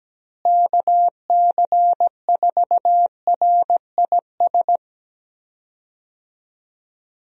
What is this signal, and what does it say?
Morse code, heard as a keyed tone, about 17 wpm, 700 Hz. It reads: KC4RIS
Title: USA Call Signs - 6 Characters - Repeated 3x - 17wpm